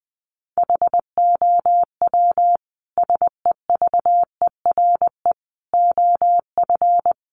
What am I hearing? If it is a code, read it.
Morse code, heard as a keyed tone, about 20 wpm, 700 Hz. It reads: HOW SEVERE OF
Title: Sentences from Top 1400 Words 20wpm